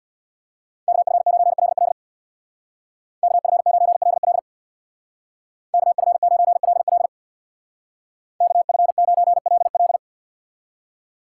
Morse code, heard as a keyed tone, about 58 wpm, 700 Hz. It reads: KF9LL KF9LL KF9LL KF9LL
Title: Speed Racing - USA 2x2 Call Signs 50wpm